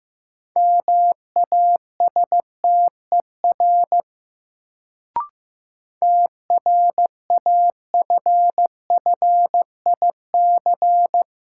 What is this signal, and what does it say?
Morse code, heard as a keyed tone, about 15 wpm, 700 Hz. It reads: MASTER TRAFFIC
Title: Sentences from Top 1300 Words - New Word Drill Alt - 15wpm